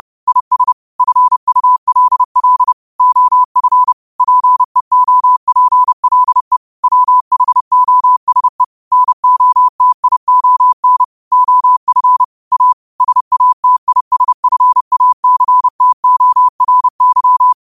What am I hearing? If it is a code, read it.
Morse code, heard as a keyed tone, about 30 wpm, 1000 Hz. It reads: IS FULL OF PEOPLE WHOSE NOTION OF A SATISFACTORY
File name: qotdfiles / 30wpm